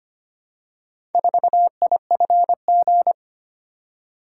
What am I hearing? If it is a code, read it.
Morse code, heard as a keyed tone, about 25 wpm, 700 Hz. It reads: 4IFG
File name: practice / 25wpm / Four Letter-Number Combinations - 25wpm